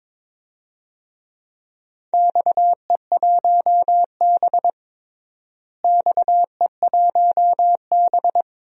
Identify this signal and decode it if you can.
Morse code, heard as a keyed tone, about 22 wpm, 700 Hz. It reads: XE1B XE1B
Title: DXCC-100 Call Signs - 4 Characters - Repeated 2x - 22wpm